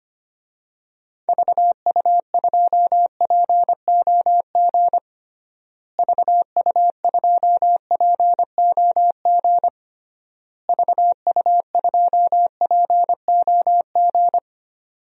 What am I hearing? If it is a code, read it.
Morse code, heard as a keyed tone, about 25 wpm, 700 Hz. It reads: VU2POG VU2POG VU2POG
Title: DXCC-100 Call Signs - 6 Characters - Repeated 3x - 25wpm